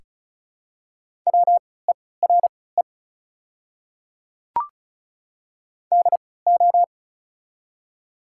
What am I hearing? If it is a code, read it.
Morse code, heard as a keyed tone, about 34 wpm, spaced as for 20 wpm, 700 Hz. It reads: WERE DO
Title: 3x Character Spacing - Top 200 Words - 35wpm